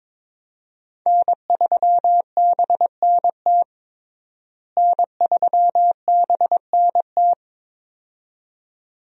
Morse code, heard as a keyed tone, about 23 wpm, 700 Hz. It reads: N3BNT N3BNT
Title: USA Call Signs - 5 Characters - Repeated 2x - 22wpm